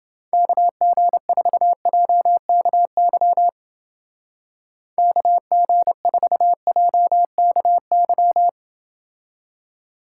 Morse code, heard as a keyed tone, about 28 wpm, 700 Hz. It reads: KG4JKY KG4JKY